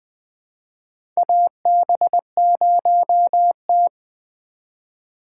Morse code, heard as a keyed tone, about 20 wpm, 700 Hz. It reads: AB0T
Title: USA Call Signs - 4 Characters - 3x Character Spacing - 20wpm